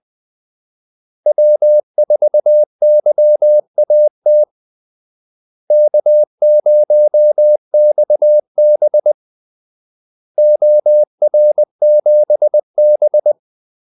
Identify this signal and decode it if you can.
Morse code, heard as a keyed tone, about 20 wpm, 600 Hz. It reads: W4YAT K0XB OR7B
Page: Kaldesignaler 20 wpm